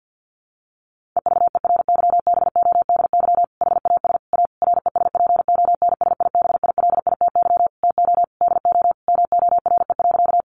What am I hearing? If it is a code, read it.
Morse code, heard as a keyed tone, about 50 wpm, 700 Hz. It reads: EVERYBODY HAS A RESPONSIBILITY TO DO MORE.
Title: Sentences from Top 1400 Words 50wpm